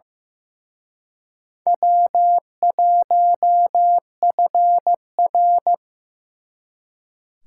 Morse code, heard as a keyed tone, about 18 wpm, 700 Hz. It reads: W1FR
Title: Speed Racing - USA 1x2 Call Signs 15wpm